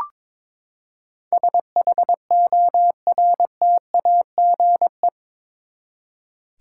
Morse code, heard as a keyed tone, about 22 wpm, 700 Hz. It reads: SHORTAGE
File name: practice / 22wpm / Eight Letter Words - 22wpm